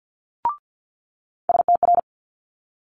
Morse code, heard as a keyed tone, about 50 wpm, 700 Hz. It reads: STR
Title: Multi-letter Phonemes Quick Fire 50wpm